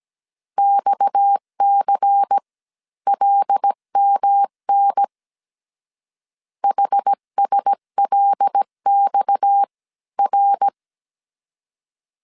Morse code, heard as a keyed tone, about 17 wpm, spaced as for 14 wpm, 800 Hz. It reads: XCLMN HSLXR
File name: CW / DET / 1 6 1500C S C 16wpm 2733